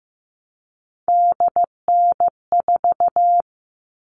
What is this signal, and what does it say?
Morse code, heard as a keyed tone, about 15 wpm, 700 Hz. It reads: DN4
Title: All Cut Numbers - 3-Digit 15wpm